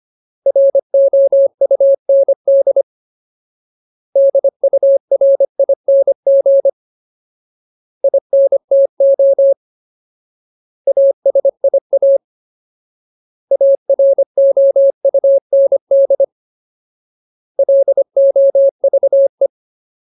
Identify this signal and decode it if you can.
Morse code, heard as a keyed tone, about 25 wpm, 550 Hz. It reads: ROUND DURING INTO ASIA AROUND LOVE
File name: Ultimate Word List / 25WPM Ultimate